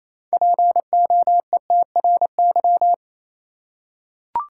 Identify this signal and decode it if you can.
Morse code, heard as a keyed tone, about 28 wpm, 700 Hz. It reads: POETRY
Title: Six Letter Words - 28wpm